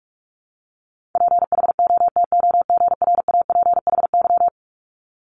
Morse code, heard as a keyed tone, about 46 wpm, 700 Hz. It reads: PHOTOGRAPHY
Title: Multi-letter Phonemes 45wpm